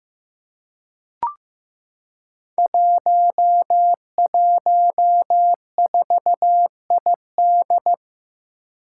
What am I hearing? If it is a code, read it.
Morse code, heard as a keyed tone, about 15 wpm, 700 Hz. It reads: 114ID
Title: Five Challenging Letter-Number Combinations 15wpm